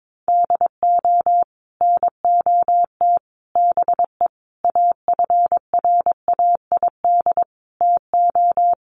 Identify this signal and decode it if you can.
Morse code, heard as a keyed tone, about 22 wpm, 700 Hz. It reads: DO NOT BE AFRAID TO
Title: Sets of 5 Words Encore2 22wpm